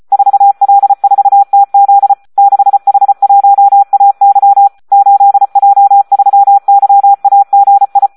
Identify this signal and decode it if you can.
Morse code, heard as a keyed tone, about 34 wpm, 800 Hz, with radio noise. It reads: 4L4TZ 6H1AY 8J3YAGI
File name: Audio practica CW / SD 1 34